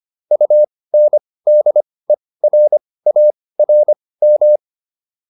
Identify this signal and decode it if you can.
Morse code, heard as a keyed tone, about 25 wpm, spaced as for 18 wpm, 600 Hz. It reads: UNDERARM